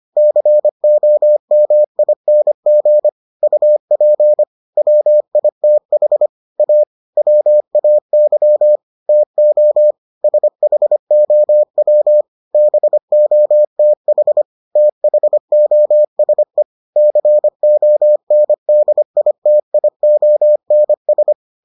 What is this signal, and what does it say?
Morse code, heard as a keyed tone, about 25 wpm, 600 Hz. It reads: COMING UP WITH A WAY TO SHOW BOTH THOSE CONDITIONS